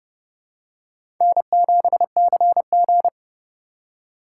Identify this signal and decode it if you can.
Morse code, heard as a keyed tone, about 30 wpm, 700 Hz. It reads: N7CG
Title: USA Call Signs - 4 Characters - 4x Character Spacing - 30wpm